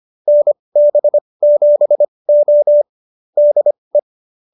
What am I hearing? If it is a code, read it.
Morse code, heard as a keyed tone, about 25 wpm, spaced as for 20 wpm, 590 Hz. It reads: NB7O DE